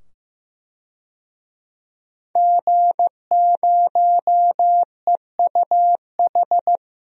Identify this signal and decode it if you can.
Morse code, heard as a keyed tone, about 15 wpm, 700 Hz. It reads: G0EUH
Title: DXCC-100 Call Signs - 5 Characters - 15wpm